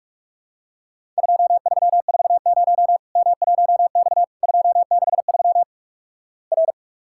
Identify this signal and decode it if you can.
Morse code, heard as a keyed tone, about 48 wpm, 700 Hz. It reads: 2340 M1X 263 R
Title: Worked All Europe DX Contest - QTC Exchange - 45wpm